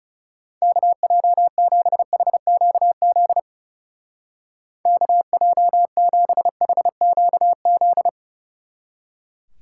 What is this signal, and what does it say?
Morse code, heard as a keyed tone, about 36 wpm, 700 Hz. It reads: KJ7HQZ KJ7HQZ
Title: Speed Racing - USA 2x3 Call Signs 30wpm